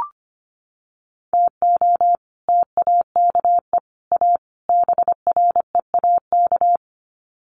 Morse code, heard as a keyed tone, about 25 wpm, 700 Hz. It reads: TO TAKE A BREAK
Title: Sets of 4 Words 25wpm